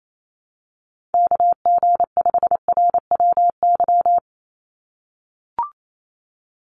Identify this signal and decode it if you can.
Morse code, heard as a keyed tone, about 28 wpm, 700 Hz. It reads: KG5RWY
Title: USA Call Signs 28wpm